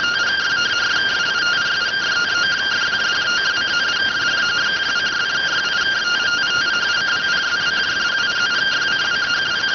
Two-tone teleprinter signal, about 170 Hz shift.